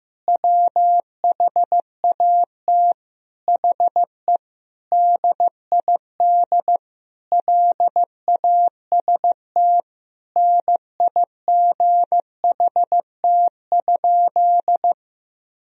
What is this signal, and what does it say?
Morse code, heard as a keyed tone, about 15 wpm, 700 Hz. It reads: WHAT HE DID LAST NIGHT?
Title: Sentences from Top 300 Words and Abbreviations 15wpm